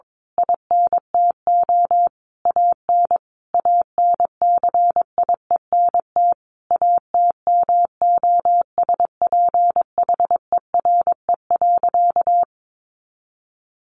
Morse code, heard as a keyed tone, about 22 wpm, 700 Hz. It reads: INTO AN ANCIENT ATMOSPHERE.
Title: A Sound Of Thunder 22wpm